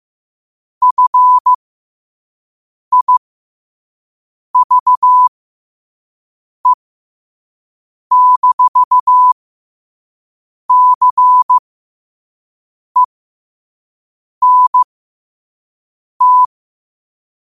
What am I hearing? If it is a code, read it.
Morse code, heard as a keyed tone, about 15 wpm, spaced as for 5 wpm, 1000 Hz. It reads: FIVE-CENT
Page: New quotes every day in morse code at 5 Words per minute.